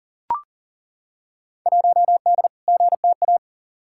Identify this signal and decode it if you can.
Morse code, heard as a keyed tone, about 40 wpm, 700 Hz. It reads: 1D GTA
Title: Field Day Exchanges - 40wpm